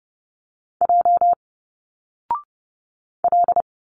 Morse code, heard as a keyed tone, about 30 wpm, 700 Hz. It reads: J L
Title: Single Letters 30wpm